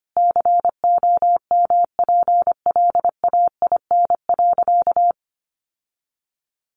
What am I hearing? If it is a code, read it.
Morse code, heard as a keyed tone, about 25 wpm, 700 Hz. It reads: COMPLAIN.
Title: Sentences from Top 1300 Words 25wpm